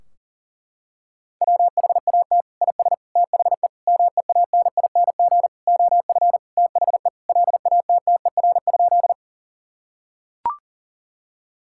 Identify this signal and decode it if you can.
Morse code, heard as a keyed tone, about 37 wpm, 700 Hz. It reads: WHAT IS THE MEANING OF THE LATTER?
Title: Sentences from Top 1600 Words 40wpm